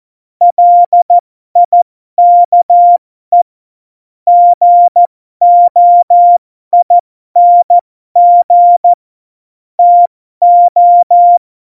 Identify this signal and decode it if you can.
Morse code, heard as a keyed tone, about 14 wpm, spaced as for 12 wpm, 700 Hz. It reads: LIKE GOING TO
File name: Morsekurs / Witze